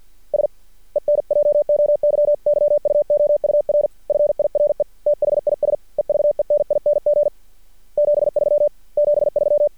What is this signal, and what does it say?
Morse code, heard as a keyed tone, about 50 wpm, 590 Hz. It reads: S ENJOYYUOUR FIRE THIS EVENING 73 73
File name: archived / CRSnet